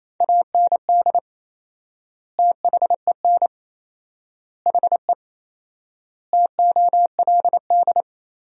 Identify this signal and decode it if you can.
Morse code, heard as a keyed tone, about 27 wpm, 700 Hz. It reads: AND THEN HE TOLD
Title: 4x Word-Spacing - Sentences from Top 400 Words - 28wpm